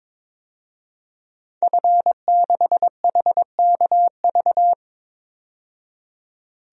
Morse code, heard as a keyed tone, about 22 wpm, 700 Hz. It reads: F6HKV
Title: DXCC-100 Call Signs - 5 Characters - 22wpm